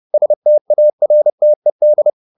Morse code, heard as a keyed tone, about 30 wpm, 600 Hz. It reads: STARTED